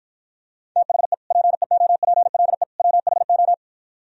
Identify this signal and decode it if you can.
Morse code, heard as a keyed tone, about 52 wpm, 700 Hz. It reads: THE PEOPLE WHO